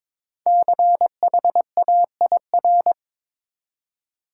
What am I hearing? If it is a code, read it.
Morse code, heard as a keyed tone, about 22 wpm, 700 Hz. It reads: CHAIR
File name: 5 Letter Words - First 3 Letters Given - 22wpm